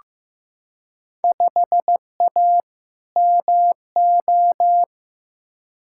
Morse code, heard as a keyed tone, about 15 wpm, 700 Hz. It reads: 5A MO
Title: Field Day Exchanges Encore - 15wpm